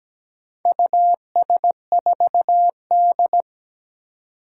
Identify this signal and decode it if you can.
Morse code, heard as a keyed tone, about 17 wpm, 700 Hz. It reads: US4D
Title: Four Challenging Letter-Number Combinations 17wpm